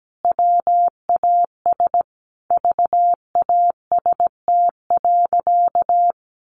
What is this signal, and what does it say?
Morse code, heard as a keyed tone, about 17 wpm, 700 Hz. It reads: WAS VAST.
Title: Sentences from Top 1300 Words 17wpm